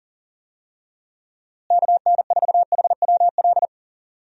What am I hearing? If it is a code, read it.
Morse code, heard as a keyed tone, about 37 wpm, 700 Hz. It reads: KN4HWL
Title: USA Call Signs - 6 Characters - Repeated 3x - 40wpm